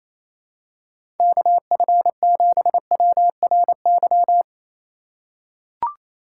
Morse code, heard as a keyed tone, about 27 wpm, 700 Hz. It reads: KF7WRY
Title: USA Call Signs - 3x Character Spacing - 28wpm